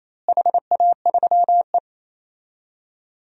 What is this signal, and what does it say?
Morse code, heard as a keyed tone, about 28 wpm, 700 Hz. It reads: HA3E
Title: DXCC-100 Call Signs - 4 Characters - Repeated 2x - 28wpm